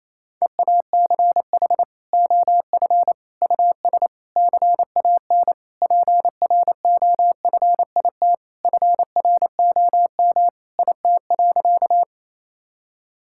Morse code, heard as a keyed tone, about 27 wpm, 700 Hz. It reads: EACH OF US CAN PROFIT FROM IT.
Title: Sentences from Top 1000 Words - Review Part 5 - 28wpm